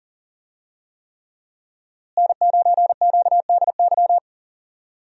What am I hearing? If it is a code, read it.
Morse code, heard as a keyed tone, about 40 wpm, 700 Hz. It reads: N9QDY
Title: USA Call Signs - 5 Characters - Repeated 3x - 40wpm